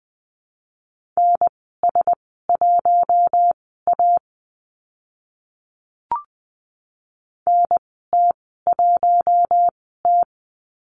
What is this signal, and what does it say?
Morse code, heard as a keyed tone, about 20 wpm, spaced as for 14 wpm, 700 Hz. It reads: NS1A NT1T
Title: 042 Lesson 8g - Number 1 - Call Signs 20wpm